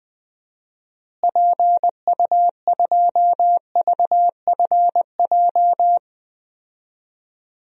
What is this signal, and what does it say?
Morse code, heard as a keyed tone, about 20 wpm, 700 Hz. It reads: PU2VFJ
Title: DXCC-100 Call Signs - 6 Characters - Repeated 2x - 20wpm